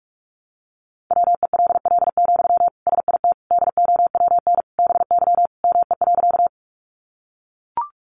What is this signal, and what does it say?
Morse code, heard as a keyed tone, about 44 wpm, 700 Hz. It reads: WELL, SIT DOWN BY ME.
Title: Sentences from Top 500 Words 45wpm